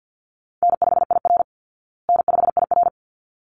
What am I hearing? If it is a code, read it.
Morse code, heard as a keyed tone, about 48 wpm, 700 Hz. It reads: N5IR N5IR
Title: Call Signs Repeated 3x - 50wpm